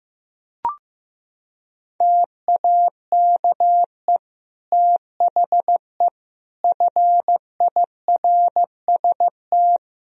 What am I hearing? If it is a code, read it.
Morse code, heard as a keyed tone, about 16 wpm, 700 Hz. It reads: TAKE THE FIRST
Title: Sentences from Top 1000 Words - Review Part 4 - 15wpm